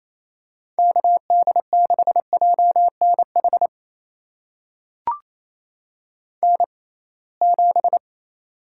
Morse code, heard as a keyed tone, about 28 wpm, spaced as for 21 wpm, 700 Hz. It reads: KD6JNH N7
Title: USA Call Signs - 6x Character Spacing - 28wpm